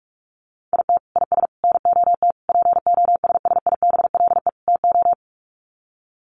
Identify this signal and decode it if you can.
Morse code, heard as a keyed tone, about 44 wpm, 700 Hz. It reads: IT IS NOT POSSIBLE TO
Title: Sets of 5 Words Encore2 45wpm